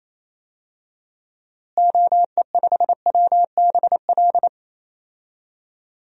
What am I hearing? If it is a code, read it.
Morse code, heard as a keyed tone, about 27 wpm, 700 Hz. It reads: OE5WBL
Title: DXCC-100 Call Signs - 6 Characters - Repeated 2x - 28wpm